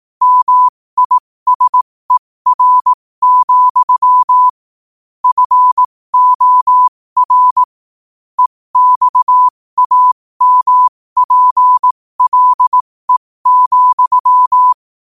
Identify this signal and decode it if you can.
Morse code, heard as a keyed tone, about 18 wpm, spaced as for 15 wpm, 1000 Hz. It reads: MISER, FOR EXAMPLE,
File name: qotdfiles / 15wpm